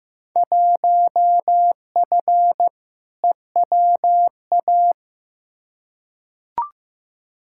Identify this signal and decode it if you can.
Morse code, heard as a keyed tone, about 15 wpm, 700 Hz. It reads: 1F EWA
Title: Field Day Exchanges Encore - 15wpm